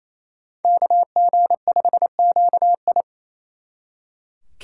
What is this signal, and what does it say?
Morse code, heard as a keyed tone, about 36 wpm, 700 Hz. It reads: KG5QI